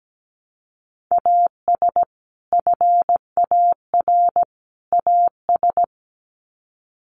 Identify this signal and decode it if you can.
Morse code, heard as a keyed tone, about 17 wpm, 700 Hz. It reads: AS FAR AS
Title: Sets of 3 Words 17wpm